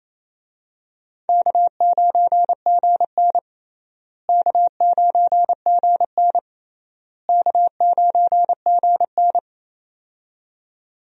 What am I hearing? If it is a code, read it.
Morse code, heard as a keyed tone, about 28 wpm, 700 Hz. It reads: K9GN K9GN K9GN
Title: USA Call Signs - 4 Characters - Repeated 3x - 28wpm